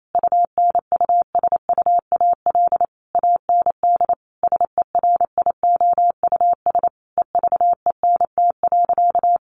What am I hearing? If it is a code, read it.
Morse code, heard as a keyed tone, about 28 wpm, 700 Hz. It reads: UNUSUAL AND SERIOUS EVENT.
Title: Sentences from Top 1400 Words 28wpm